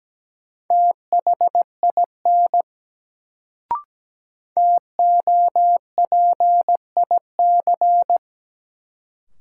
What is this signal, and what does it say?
Morse code, heard as a keyed tone, about 17 wpm, 700 Hz. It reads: THIN TOPIC
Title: Sentences from Top 1200 Words - New Word Drill - 17wpm